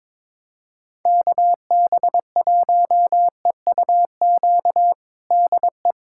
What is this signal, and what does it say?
Morse code, heard as a keyed tone, about 22 wpm, 700 Hz. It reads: KB1EUQ DE
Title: USA Callsign DE Callsign - 6 Characters - 22wpm